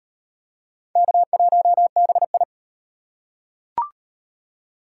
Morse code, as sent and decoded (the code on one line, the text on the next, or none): -.- .---- -... ..
K1BI